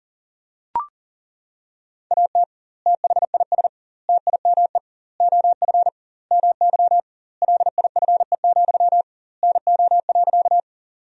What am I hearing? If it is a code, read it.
Morse code, heard as a keyed tone, about 37 wpm, 700 Hz. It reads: AT THIS TIME OF MY LIFE, NO.
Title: 2x Word-Spacing - Sentences from Top 300 Words - 40wpm